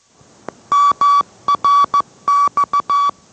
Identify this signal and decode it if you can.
Morse code, heard as a keyed tone, about 14 wpm, 1140 Hz. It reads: MRX